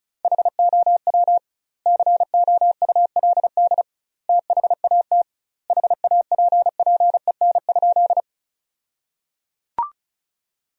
Morse code, heard as a keyed tone, about 35 wpm, 700 Hz. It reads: HOW COULD THAT HAPPEN?
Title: 2x Word-Spacing - Sentences from Top 400 Words - 35wpm